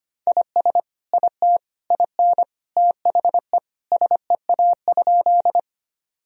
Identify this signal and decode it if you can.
Morse code, heard as a keyed tone, about 25 wpm, 700 Hz. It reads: IS IT IN THE SEA?
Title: Sentences from Top 1000 Words - Review Part 1 - 25wpm